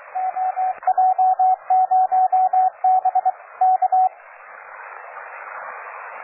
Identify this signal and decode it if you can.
Morse code, heard as a keyed tone, about 26 wpm, 750 Hz, with radio noise. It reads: OJ0B K
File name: cw forum / OJ0B sur 1823 le 21-09-2008 17h00 test GP XXL